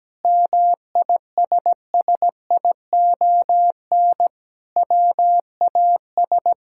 Morse code, heard as a keyed tone, about 17 wpm, 700 Hz. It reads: MISSION WAS
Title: Sentences from Top 1600 Words 17wpm